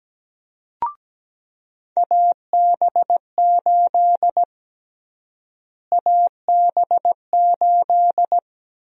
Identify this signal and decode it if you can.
Morse code, heard as a keyed tone, about 17 wpm, 700 Hz. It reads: AB8 AB8
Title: USA Call Sign Prefixes - Repeated 2x - 17wpm